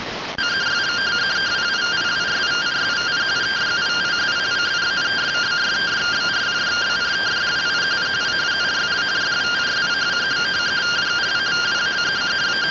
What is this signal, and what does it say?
Two-tone teleprinter signal, about 170 Hz shift.